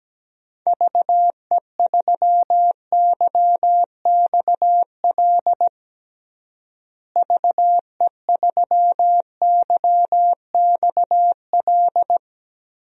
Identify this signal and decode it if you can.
Morse code, heard as a keyed tone, about 17 wpm, 700 Hz. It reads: VE3YXL VE3YXL
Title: DXCC-100 Call Signs - 6 Characters - Repeated 2x - 17wpm